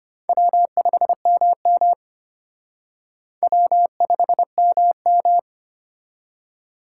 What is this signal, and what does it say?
Morse code, heard as a keyed tone, about 33 wpm, 700 Hz. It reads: W5MM W5MM